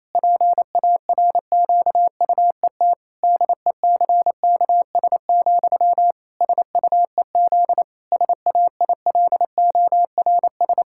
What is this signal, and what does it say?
Morse code, heard as a keyed tone, about 28 wpm, 700 Hz. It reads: PARQUET DECKS, SUEZ SAILORS
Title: Pangram Challenge v2 - 28wpm